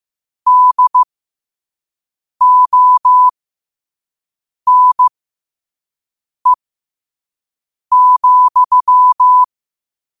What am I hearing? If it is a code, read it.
Morse code, heard as a keyed tone, about 15 wpm, spaced as for 5 wpm, 1000 Hz. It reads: DONE,
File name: qotdfiles / 5wpm